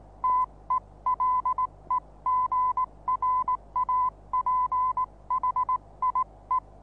Morse code, heard as a keyed tone, about 19 wpm, 1010 Hz. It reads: TELEGRAPHIE
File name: repos / Morsekurs / DJ4UF Morsen26